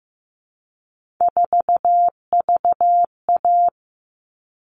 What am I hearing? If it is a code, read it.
Morse code, heard as a keyed tone, about 15 wpm, 700 Hz. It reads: 4VA
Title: All Cut Numbers - 3-Digit 15wpm